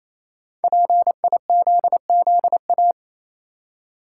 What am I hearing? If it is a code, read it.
Morse code, heard as a keyed tone, about 27 wpm, 700 Hz. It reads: PIZZA
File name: Easy Words - Speed-Racing with Hints - 22wpm